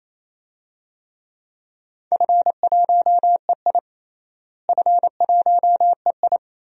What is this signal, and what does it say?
Morse code, heard as a keyed tone, about 28 wpm, 700 Hz. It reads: F1EI F1EI
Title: DXCC-100 Call Signs - 4 Characters - Repeated 2x - 28wpm